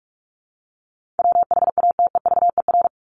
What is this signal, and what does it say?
Morse code, heard as a keyed tone, about 44 wpm, 700 Hz. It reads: WHATEVER
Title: Top 1001-1250 Words 45wpm